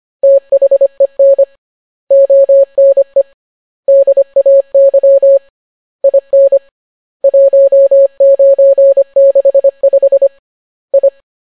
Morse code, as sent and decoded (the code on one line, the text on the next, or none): - .... -. / --- -. / -.. .- -.-- / .. -. / .---- ----. -.... ..... / ..
THEN ONE DAY IN 1965 I